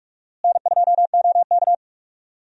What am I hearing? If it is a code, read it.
Morse code, heard as a keyed tone, about 47 wpm, 700 Hz. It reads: N2OK
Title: USA Call Signs - 4 Characters - Repeated 3x - 45wpm